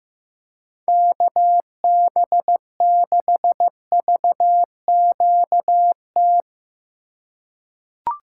Morse code, heard as a keyed tone, about 15 wpm, 700 Hz. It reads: KB6VQT
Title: USA Call Signs - 6 Characters - 3x Character Spacing - 15wpm